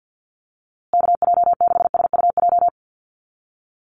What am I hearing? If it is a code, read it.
Morse code, heard as a keyed tone, about 50 wpm, 700 Hz. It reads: KJ6SUJ
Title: USA Call Signs Repeated 2x 50wpm